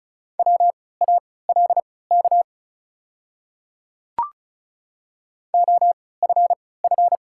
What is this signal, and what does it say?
Morse code, heard as a keyed tone, about 34 wpm, spaced as for 20 wpm, 700 Hz. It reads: WALK OFF
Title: 3x Character Spacing - Top 300 Words - 35wpm